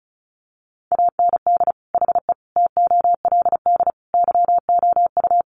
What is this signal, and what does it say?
Morse code, heard as a keyed tone, about 34 wpm, 700 Hz. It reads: AND HE TOLD YOU
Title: Sentences from Top 500 Words 35wpm